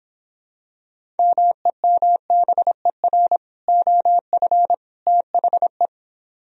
Morse code, heard as a keyed tone, about 25 wpm, 700 Hz. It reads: MEMBER OF THE